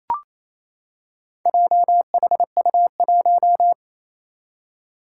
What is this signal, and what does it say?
Morse code, heard as a keyed tone, about 28 wpm, 700 Hz. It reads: JHU1
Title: Four Challenging Letter-Number Combinations 28wpm